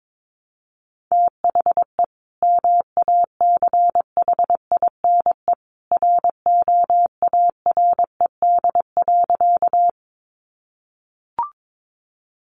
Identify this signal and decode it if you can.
Morse code, heard as a keyed tone, about 22 wpm, 700 Hz. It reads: THE MACHINE ROARED.
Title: A Sound Of Thunder 22wpm